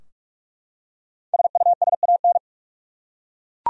Morse code, heard as a keyed tone, about 43 wpm, 700 Hz. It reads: SUSAN
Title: USA Common Names - 45wpm